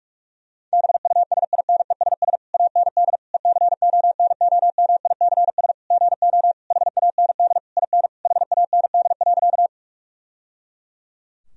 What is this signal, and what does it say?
Morse code, heard as a keyed tone, about 46 wpm, 700 Hz. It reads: BUSINESS AND ECONOMICS GO HAND IN HAND.
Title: Sentences from Top 1000 Words - Review Part 3 - 45wpm